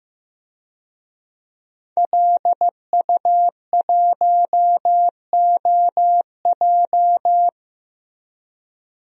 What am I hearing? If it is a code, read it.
Morse code, heard as a keyed tone, about 15 wpm, 700 Hz. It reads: LU1OJ
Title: DXCC-100 Call Signs - 5 Characters - Repeated 2x - 15wpm